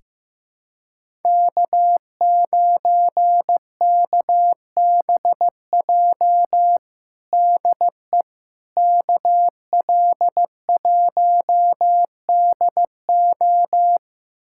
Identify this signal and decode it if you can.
Morse code, heard as a keyed tone, about 15 wpm, 700 Hz. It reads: K9KBJ DE KL1DO
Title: USA Callsign DE Callsign - 5 Characters - 15wpm